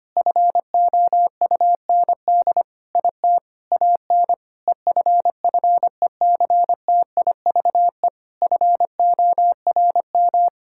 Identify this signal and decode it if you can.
Morse code, heard as a keyed tone, about 25 wpm, 700 Hz. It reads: FOUND IT AN EFFECTIVE FORM
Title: Sentences from Top 1000 Words - Review Part 6 - 25wpm